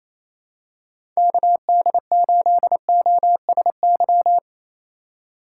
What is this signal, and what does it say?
Morse code, heard as a keyed tone, about 28 wpm, 700 Hz. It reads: KD8OSY
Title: USA Call Signs - 6 Characters - 3x Character Spacing - 28wpm